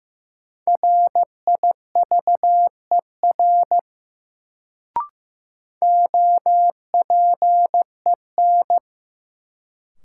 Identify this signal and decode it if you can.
Morse code, heard as a keyed tone, about 15 wpm, 700 Hz. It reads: RIVER OPEN
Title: Sentences from Top 300 Words - New Word Drill - 15wpm